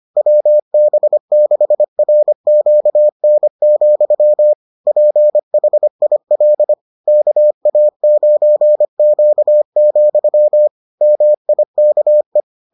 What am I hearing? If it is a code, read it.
Morse code, heard as a keyed tone, about 25 wpm, 600 Hz. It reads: WB6RQN, PHIL KA9Q, MIKE